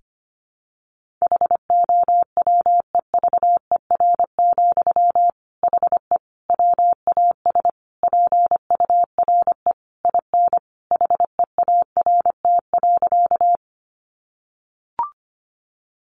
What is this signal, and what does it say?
Morse code, heard as a keyed tone, about 25 wpm, 700 Hz. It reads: HOWEVER, HE WAS PURE IN HEART.
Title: Sentences from Top 1500 Words 25wpm